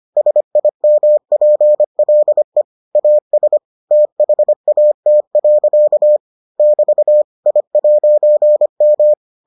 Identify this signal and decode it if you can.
Morse code, heard as a keyed tone, about 25 wpm, 600 Hz. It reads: SIMPLE AS THAT. <BT> I'M